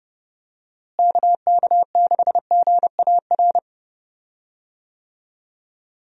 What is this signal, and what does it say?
Morse code, heard as a keyed tone, about 30 wpm, 700 Hz. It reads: KK6GAR
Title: USA Call Signs - 6 Characters - Repeated 2x - 30wpm